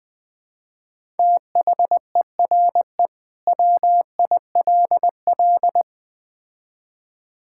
Morse code, heard as a keyed tone, about 20 wpm, 700 Hz. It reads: THERE WILL
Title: Build A Phrase - 3-Word Phrases - 20wpm